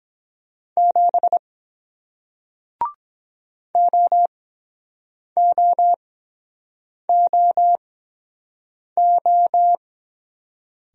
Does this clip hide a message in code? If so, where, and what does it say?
Yes: Morse code from 0.8 to 9.8 s, heard as a keyed tone, about 23 wpm, spaced as for 7 wpm, 700 Hz. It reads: 7 OOOO